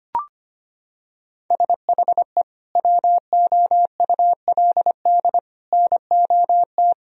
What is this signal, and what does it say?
Morse code, heard as a keyed tone, about 25 wpm, 700 Hz. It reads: SHE WOULD NOT
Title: Sentences from Top 1000 Words - Review Part 3 - 25wpm